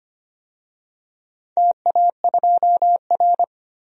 Morse code, heard as a keyed tone, about 25 wpm, 700 Hz. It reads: TA2R
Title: DXCC-100 Call Signs - 4 Characters - 25wpm